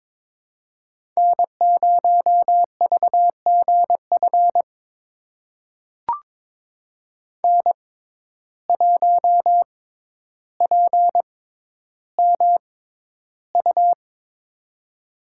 Morse code, heard as a keyed tone, about 22 wpm, spaced as for 11 wpm, 700 Hz. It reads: N0VGF N1PMU
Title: USA Call Signs - 5 Characters - 6x Character Spacing - 22wpm